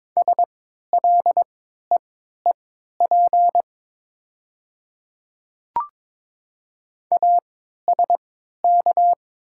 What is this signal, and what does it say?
Morse code, heard as a keyed tone, about 22 wpm, spaced as for 13 wpm, 700 Hz. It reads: SLEEP ASK
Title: 3x Character Spacing - Top 500 Words - 22wpm